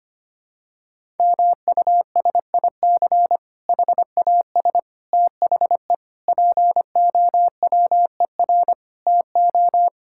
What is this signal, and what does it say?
Morse code, heard as a keyed tone, about 25 wpm, 700 Hz. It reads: MUSIC HAS THE POWER TO
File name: practice / 25wpm / Difficult Sentences with Hints - 25wpm